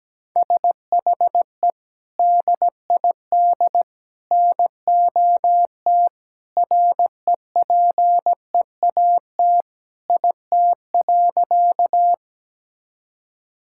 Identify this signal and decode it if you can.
Morse code, heard as a keyed tone, about 17 wpm, 700 Hz. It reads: SHE DID NOT REPEAT IT.
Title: Sentences from Top 1600 Words 17wpm